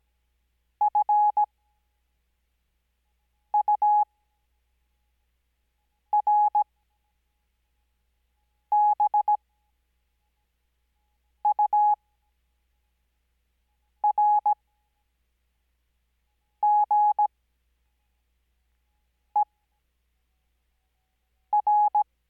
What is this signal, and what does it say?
Morse code, heard as a keyed tone, about 17 wpm, spaced as for 4 wpm, 840 Hz. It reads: FURBURGER
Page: Farnsworth Spacing: 15WPM Character speed Spaced at 5WPM.